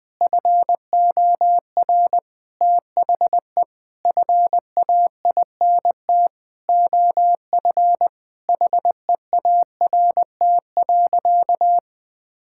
Morse code, heard as a keyed tone, about 20 wpm, 700 Hz. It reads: FOR THE FAINT OF HEART.
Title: Sentences with 7-Word Phrases - 20wpm